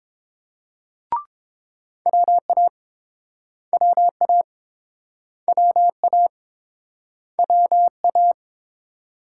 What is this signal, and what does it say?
Morse code, heard as a keyed tone, about 30 wpm, 700 Hz. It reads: WA WA WA WA